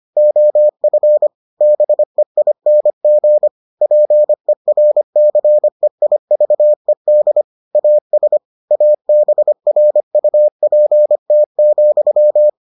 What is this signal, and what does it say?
Morse code, heard as a keyed tone, about 25 wpm, 600 Hz. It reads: OF BEING PERCEIVED AS ABRUPT,